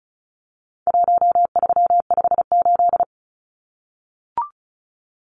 Morse code, heard as a keyed tone, about 34 wpm, 700 Hz. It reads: 1358
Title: Numbers - Contest Style - 4-Digit 35wpm